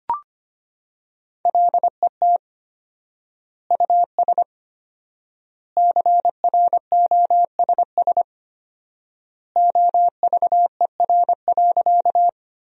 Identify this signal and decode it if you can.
Morse code, heard as a keyed tone, about 25 wpm, 700 Hz. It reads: LET US CROSS OVER.
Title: 4x Word-Spacing - Sentences from Top 300 Words - 25wpm